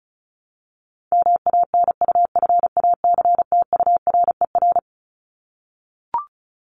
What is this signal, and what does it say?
Morse code, heard as a keyed tone, about 34 wpm, 700 Hz. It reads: MANUFACTURER
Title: Top 1751-2000 Words 35wpm